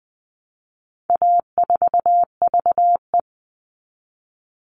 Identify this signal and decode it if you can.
Morse code, heard as a keyed tone, about 20 wpm, 700 Hz. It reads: A4VE
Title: All Cut Numbers - 4-Digit 20wpm